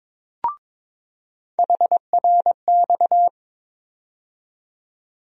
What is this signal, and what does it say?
Morse code, heard as a keyed tone, about 23 wpm, 700 Hz. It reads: HRX
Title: USA Call Sign Postfixes - 3 Letters 22wpm